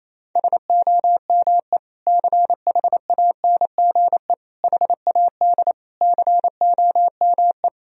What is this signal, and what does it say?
Morse code, heard as a keyed tone, about 28 wpm, 700 Hz. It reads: SOME CHANGE HAD COME
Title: Sentences from Top 1000 Words - Review Part 2 - 28wpm